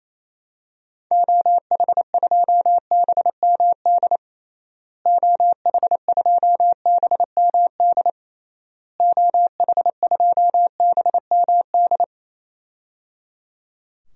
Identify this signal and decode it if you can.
Morse code, heard as a keyed tone, about 28 wpm, 700 Hz. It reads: OH2BMD OH2BMD OH2BMD
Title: DXCC-100 Call Signs - 6 Characters - Repeated 3x - 28wpm